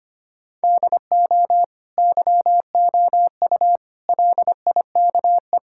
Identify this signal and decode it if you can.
Morse code, heard as a keyed tone, about 25 wpm, 700 Hz. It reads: DO YOU LIKE
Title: Sentences from Top 1000 Words - Review Part 6 - 25wpm